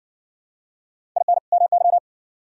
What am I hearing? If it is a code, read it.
Morse code, heard as a keyed tone, about 60 wpm, 700 Hz. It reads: IN MY